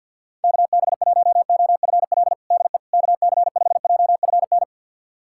Morse code, heard as a keyed tone, about 47 wpm, 700 Hz. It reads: KB1OFL DE KC5JFN
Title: USA Callsign DE Callsign - 6 Characters - 50wpm